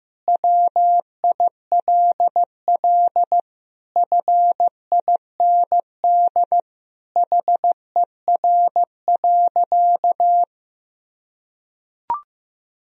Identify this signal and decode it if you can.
Morse code, heard as a keyed tone, about 14 wpm, 700 Hz. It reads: WILL FIND HER.
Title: Sentences from Top 1000 Words - Review Part 2 - 15wpm